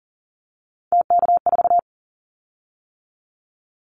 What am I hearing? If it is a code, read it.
Morse code, heard as a keyed tone, about 39 wpm, 690 Hz. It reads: TK4
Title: Callsign Prefixes for Top 100 DXCC - 40wpm